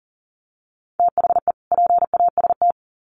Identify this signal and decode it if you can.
Morse code, heard as a keyed tone, about 40 wpm, 700 Hz. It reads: THE PAST
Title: Sets of 2 Words 40wpm